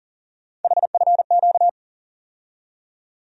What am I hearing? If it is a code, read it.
Morse code, heard as a keyed tone, about 40 wpm, 700 Hz. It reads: HFQ
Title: USA Call Sign Postfixes - 3 Letters 40wpm